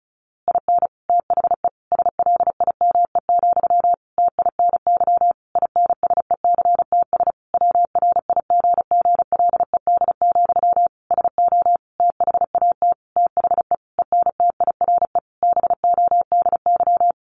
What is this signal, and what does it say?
Morse code, heard as a keyed tone, about 34 wpm, 700 Hz. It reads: IN THE SLIME, TINY INSECTS WRIGGLED, SO THAT THE ENTIRE BODY
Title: A Sound Of Thunder 35wpm